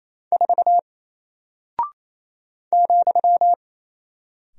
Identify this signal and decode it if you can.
Morse code, heard as a keyed tone, about 27 wpm, 700 Hz. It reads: 4 ,
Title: Single Character (Letter:Number:Punctuation:Prosign) Alt 28wpm